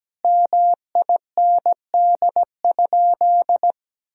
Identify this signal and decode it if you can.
Morse code, heard as a keyed tone, about 17 wpm, 700 Hz. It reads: MIND?
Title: Sentences from Top 1000 Words - Review Part 2 - 17wpm